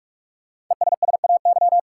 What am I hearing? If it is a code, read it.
Morse code, heard as a keyed tone, about 47 wpm, 700 Hz. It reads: ESSAY
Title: Sentences from Top 1500 Words - New Word Drill Alt - 45wpm